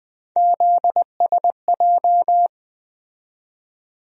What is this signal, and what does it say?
Morse code, heard as a keyed tone, about 20 wpm, 700 Hz. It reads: ZSJ
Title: USA Call Sign Postfixes - 3 Letters 20wpm